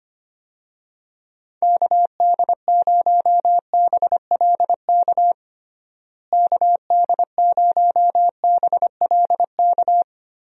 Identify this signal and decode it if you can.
Morse code, heard as a keyed tone, about 25 wpm, 700 Hz. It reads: KD0BLK KD0BLK
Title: USA Call Signs - 6 Characters - Repeated 3x - 25wpm